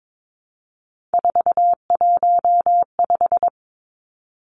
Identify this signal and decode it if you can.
Morse code, heard as a keyed tone, about 22 wpm, 700 Hz. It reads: 415
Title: Numbers - 3-Digit 22wpm